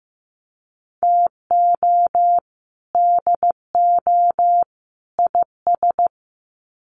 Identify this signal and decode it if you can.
Morse code, heard as a keyed tone, about 15 wpm, 700 Hz. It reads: TO DO IS
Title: Sets of 3 Words 15wpm